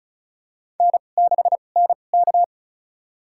Morse code, heard as a keyed tone, about 34 wpm, spaced as for 25 wpm, 700 Hz. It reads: N6NK
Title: USA Call Signs - 4 Characters - 2x Character Spacing - 35wpm